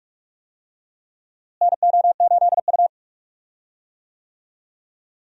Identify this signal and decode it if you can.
Morse code, heard as a keyed tone, about 42 wpm, 700 Hz. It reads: NO8U
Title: USA Call Signs - 4 Characters - Repeated 3x - 45wpm